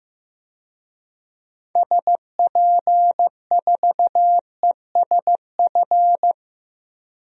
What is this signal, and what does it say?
Morse code, heard as a keyed tone, about 15 wpm, 700 Hz. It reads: SP4ESF
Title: DXCC-100 Call Signs - 6 Characters - Repeated 2x - 15wpm